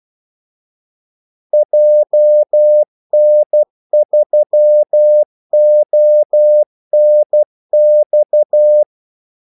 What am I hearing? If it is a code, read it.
Morse code, heard as a keyed tone, about 12 wpm, 600 Hz. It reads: JN3ONX
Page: 2 ekstra ordmellemrum mellem kaldesignalerne